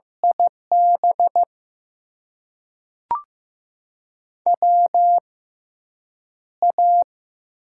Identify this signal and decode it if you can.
Morse code, heard as a keyed tone, about 15 wpm, spaced as for 8 wpm, 700 Hz. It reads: IB WA
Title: USA Call Signs - 6x Character Spacing - 15wpm